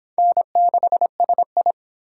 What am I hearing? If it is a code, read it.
Morse code, heard as a keyed tone, about 24 wpm, 700 Hz. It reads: N6SI